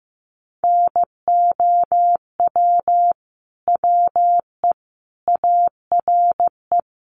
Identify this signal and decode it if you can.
Morse code, heard as a keyed tone, about 15 wpm, 700 Hz. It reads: NOW WE ARE
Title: Sentences from Top 1200 Words 15wpm